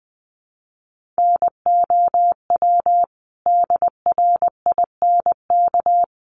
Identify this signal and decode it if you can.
Morse code, heard as a keyed tone, about 20 wpm, 700 Hz. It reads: NOW DRINK
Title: Sentences from Top 1500 Words 20wpm